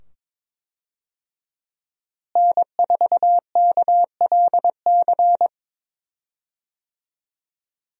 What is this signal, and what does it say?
Morse code, heard as a keyed tone, about 21 wpm, 700 Hz. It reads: N4KLC
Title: USA Call Signs - 5 Characters - Repeated 2x - 22wpm